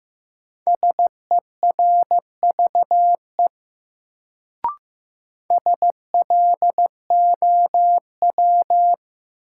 Morse code, heard as a keyed tone, about 15 wpm, 700 Hz. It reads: SERVE SLOW
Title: Sentences from Top 400 Words - New Word Drill - 15wpm